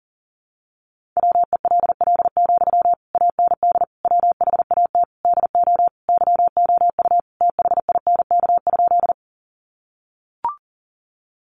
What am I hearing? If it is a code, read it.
Morse code, heard as a keyed tone, about 40 wpm, 700 Hz. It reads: WELL, AND WHAT DO YOU THINK?
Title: Sentences from Top 200 Words 40wpm